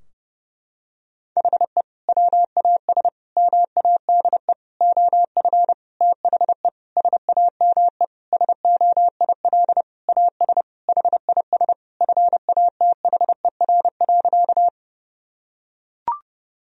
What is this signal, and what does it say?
Morse code, heard as a keyed tone, about 30 wpm, 700 Hz. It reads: HE WAS MADE OF THE SAME SOIL AS HIS FATHER.
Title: Sentences from Top 1000 Words - Review Part 4 - 30wpm